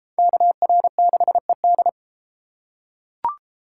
Morse code, heard as a keyed tone, about 31 wpm, 700 Hz. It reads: KR6ED